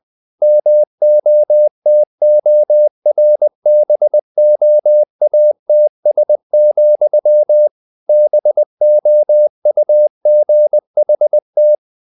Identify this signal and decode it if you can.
Morse code, heard as a keyed tone, about 20 wpm, 600 Hz. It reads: MOTORBOATS, BOUGHT
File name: the old man and the sea / the old man and the sea part 1 20wpm